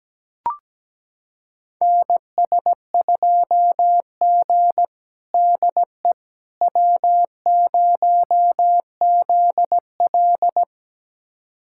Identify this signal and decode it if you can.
Morse code, heard as a keyed tone, about 17 wpm, 700 Hz. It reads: NS2G DE W0ZL
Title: USA Callsign DE Callsign - 4 Characters - 17wpm